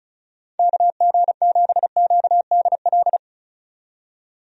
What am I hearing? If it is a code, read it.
Morse code, heard as a keyed tone, about 34 wpm, 700 Hz. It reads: KG7QDL
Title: USA Call Signs - 6 Characters - 35wpm